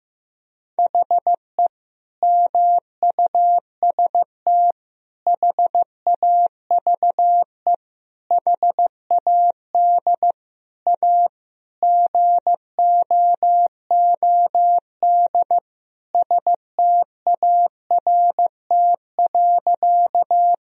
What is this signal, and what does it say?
Morse code, heard as a keyed tone, about 15 wpm, 700 Hz. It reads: HE MUST HAVE HAD A GOOD START.
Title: Sentences from Top 1000 Words - Review Part 1 - 15wpm